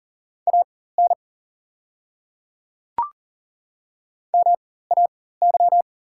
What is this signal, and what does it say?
Morse code, heard as a keyed tone, about 40 wpm, spaced as for 19 wpm, 690 Hz. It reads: AN MAY
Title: 4x Character Spacing - Top 100 Words - 40wpm